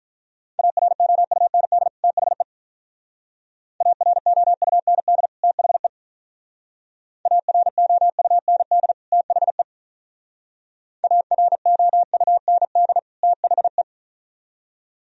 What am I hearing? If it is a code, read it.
Morse code, heard as a keyed tone, about 44 wpm, 700 Hz. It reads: AROUND THE AROUND THE AROUND THE AROUND THE